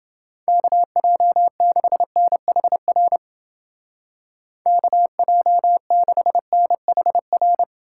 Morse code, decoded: KJ6NHR KJ6NHR

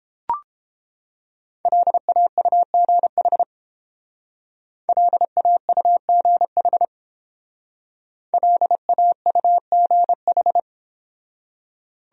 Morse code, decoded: LAUGH LAUGH LAUGH

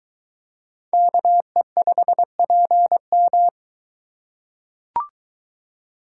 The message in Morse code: KE5PM

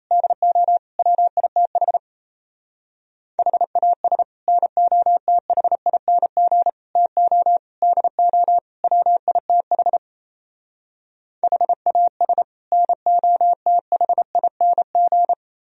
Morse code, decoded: DO WITH HAS NOTHING TO DO WITH HAS NOTHING

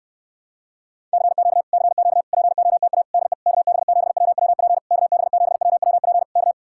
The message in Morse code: CQ CQ POTA DE KB7WWP KB7WWP K

